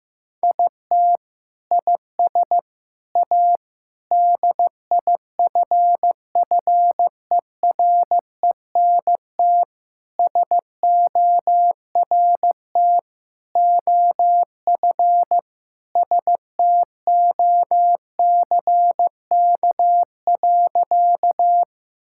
Morse code, heard as a keyed tone, about 15 wpm, 700 Hz. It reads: IT IS A DIFFERENT SORT OF STOCK.
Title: Sentences from Top 1000 Words - Review Part 5 - 15wpm